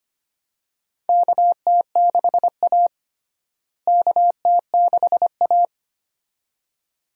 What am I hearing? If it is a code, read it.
Morse code, heard as a keyed tone, about 25 wpm, 700 Hz. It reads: KT6A KT6A
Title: USA Call Signs - 4 Characters - Repeated 2x - 25wpm